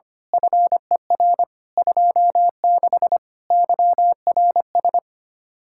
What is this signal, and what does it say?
Morse code, heard as a keyed tone, about 24 wpm, 700 Hz. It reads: FER 26 YRS
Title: Speed-Racing Common QSO Phrases - 25wpm